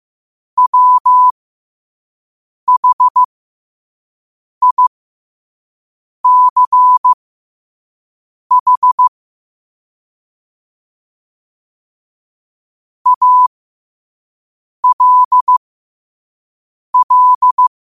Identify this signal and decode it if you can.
Morse code, heard as a keyed tone, about 15 wpm, spaced as for 5 wpm, 1000 Hz. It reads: WHICH ALL
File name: 5wpm